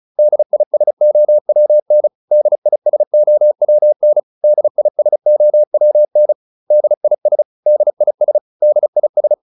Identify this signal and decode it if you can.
Morse code, heard as a keyed tone, about 34 wpm, 600 Hz. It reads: DISOWN DISOWN DISOWN DIS DIS DIS
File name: DIS6 35